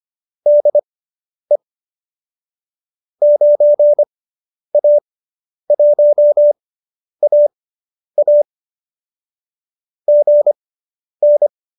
Morse code, heard as a keyed tone, about 25 wpm, spaced as for 10 wpm, 590 Hz. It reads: DE 9A1AA GN